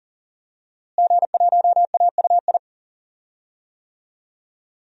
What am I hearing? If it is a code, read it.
Morse code, heard as a keyed tone, about 40 wpm, 700 Hz. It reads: G1AUI
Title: DXCC-100 Call Signs - 5 Characters - Repeated 2x - 40wpm